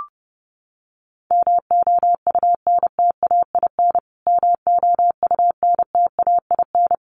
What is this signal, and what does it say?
Morse code, heard as a keyed tone, about 30 wpm, 700 Hz. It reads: MOUNTAIN MOUNTAIN
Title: Top 300 Words Repeated 30wpm